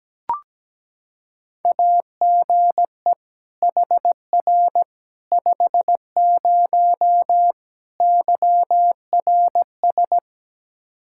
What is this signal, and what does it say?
Morse code, heard as a keyed tone, about 17 wpm, 700 Hz. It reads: AGE HR 50 YRS
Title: K1USN SST QSO Practice 17wpm